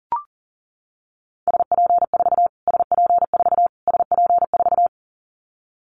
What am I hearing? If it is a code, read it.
Morse code, heard as a keyed tone, about 40 wpm, 700 Hz. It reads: SP4 SP4 SP4
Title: Callsign Prefixes for Top 25 DXCC - Repeated 3x - 40wpm